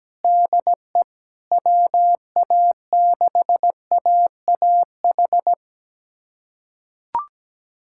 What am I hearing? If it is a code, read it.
Morse code, heard as a keyed tone, about 17 wpm, 700 Hz. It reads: DE WA6AAH
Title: USA Callsign DE Callsign 17wpm